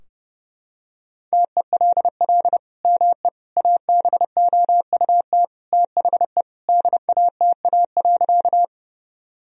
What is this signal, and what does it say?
Morse code, heard as a keyed tone, about 32 wpm, 700 Hz. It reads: TELL ME ABOUT THE DATA.
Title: Sentences from Top 1000 Words - Review Part 3 - 30wpm